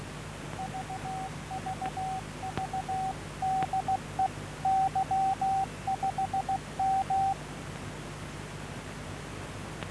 Morse code, heard as a keyed tone, about 16 wpm, 760 Hz, with noise. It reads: VVV DE Y5M